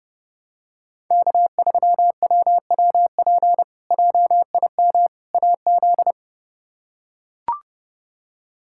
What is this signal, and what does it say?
Morse code, heard as a keyed tone, about 32 wpm, 700 Hz. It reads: K3WWP JIM AZ
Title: K1USN SST QSO Practice - 30wpm